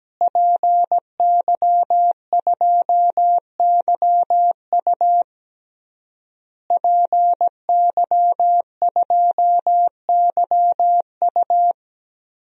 Morse code, heard as a keyed tone, about 17 wpm, 700 Hz. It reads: PY2YU PY2YU
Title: DXCC-100 Call Signs - 5 Characters - Repeated 3x - 17wpm